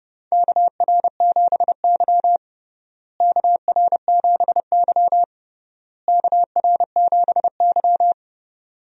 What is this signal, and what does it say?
Morse code, heard as a keyed tone, about 32 wpm, 700 Hz. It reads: KR7Y KR7Y KR7Y
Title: USA Call Signs - 4 Characters - Repeated 3x - 30wpm